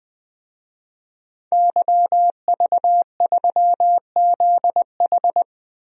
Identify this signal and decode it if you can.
Morse code, heard as a keyed tone, about 20 wpm, 700 Hz. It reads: YV3ZH
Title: DXCC-100 Call Signs - 5 Characters - 20wpm